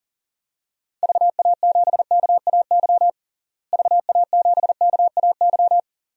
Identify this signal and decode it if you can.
Morse code, heard as a keyed tone, about 40 wpm, 700 Hz. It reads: VA7KAY VA7KAY
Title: DXCC-100 Call Signs - 6 Characters - Repeated 2x - 40wpm